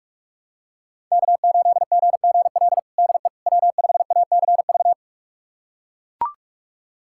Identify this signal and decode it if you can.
Morse code, heard as a keyed tone, about 42 wpm, 700 Hz. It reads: K8GGL DE W5ACV
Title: USA Callsign DE Callsign - 5 Characters - 45wpm